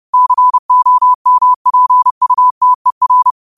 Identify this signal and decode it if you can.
Morse code, heard as a keyed tone, about 30 wpm, 1000 Hz. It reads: COMPUTER